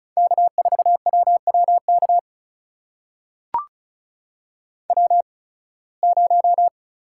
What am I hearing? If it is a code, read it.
Morse code, heard as a keyed tone, about 34 wpm, spaced as for 23 wpm, 700 Hz. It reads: K4WWK W0
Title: USA Call Signs - 5 Characters - 8x Character Spacing - 35wpm